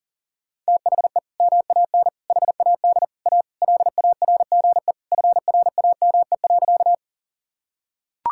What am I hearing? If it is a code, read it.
Morse code, heard as a keyed tone, about 40 wpm, 700 Hz. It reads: THE MAN HAD A LARGE FRAME.
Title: Sentences from Top 1000 Words - Review Part 6 - 40wpm